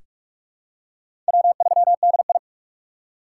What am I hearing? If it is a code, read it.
Morse code, heard as a keyed tone, about 46 wpm, 700 Hz. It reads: W3DI
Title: USA Call Signs - 2x Character Spacing - 45wpm